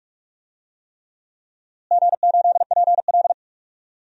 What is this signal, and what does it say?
Morse code, heard as a keyed tone, about 42 wpm, 700 Hz. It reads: G8PL
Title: DXCC-100 Call Signs - 4 Characters - Repeated 3x - 45wpm